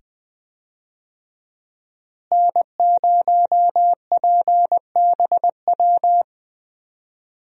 Morse code, heard as a keyed tone, about 20 wpm, 700 Hz. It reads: N0PBW
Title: USA Call Signs - 5 Characters - Repeated 2x - 20wpm